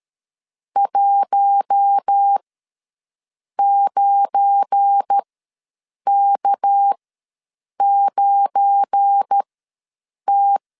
Morse code, heard as a keyed tone, about 13 wpm, spaced as for 7 wpm, 800 Hz. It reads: 19K9T
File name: LEZIONI / 7 GF9Z / 7 GF9 ws 12wpm 3344